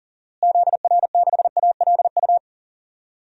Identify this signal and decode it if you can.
Morse code, heard as a keyed tone, about 40 wpm, 700 Hz. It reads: ZR6ALU
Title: DXCC-100 Call Signs - 6 Characters - Repeated 3x - 40wpm